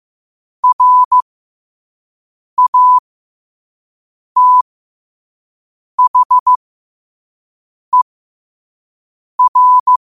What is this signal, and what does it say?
Morse code, heard as a keyed tone, about 15 wpm, spaced as for 5 wpm, 1000 Hz. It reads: RATHER
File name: qotdfiles / 5wpm